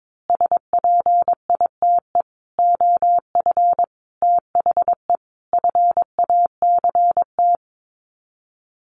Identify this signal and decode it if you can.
Morse code, heard as a keyed tone, about 22 wpm, 700 Hz. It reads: SPITE OF THE FACT
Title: Sets of 5 Words Encore2 22wpm